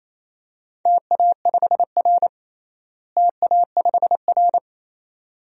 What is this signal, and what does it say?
Morse code, heard as a keyed tone, about 28 wpm, 700 Hz. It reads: TA5R TA5R
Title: DXCC-100 Call Signs - 4 Characters - Repeated 2x - 28wpm